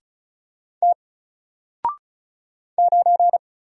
Morse code, heard as a keyed tone, about 36 wpm, 700 Hz. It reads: T 9
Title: Single Letter-Number Alt 35wpm